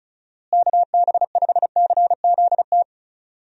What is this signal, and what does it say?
Morse code, heard as a keyed tone, about 34 wpm, 700 Hz. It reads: KB5CZT
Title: USA Call Signs - 6 Characters - Repeated 3x - 35wpm